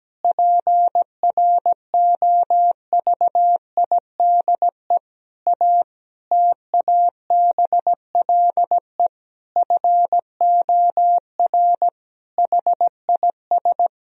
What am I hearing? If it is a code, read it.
Morse code, heard as a keyed tone, about 17 wpm, 700 Hz. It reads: PROVIDE A TABLE FOR HIS
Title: Sentences from Top 1000 Words - Review Part 3 - 17wpm